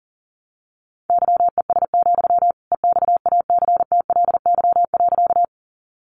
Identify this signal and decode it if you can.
Morse code, heard as a keyed tone, about 37 wpm, 700 Hz. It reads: YES, EXACTLY.
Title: Sentences from Top 900 Words 40wpm